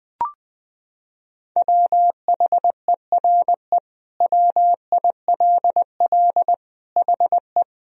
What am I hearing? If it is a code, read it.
Morse code, heard as a keyed tone, about 20 wpm, 700 Hz. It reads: WHERE WILL HE
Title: Sentences from Top 1000 Words - Review Part 2 - 20wpm